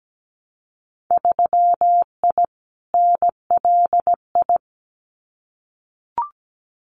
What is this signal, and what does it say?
Morse code, heard as a keyed tone, about 17 wpm, 700 Hz. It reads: 3I NLI
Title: Winter Field Day Exchanges 17wpm